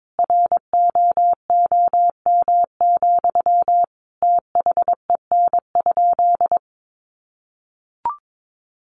Morse code, heard as a keyed tone, about 22 wpm, 700 Hz. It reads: ROOM, THEN?
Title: Sentences from Top 600 Words 22wpm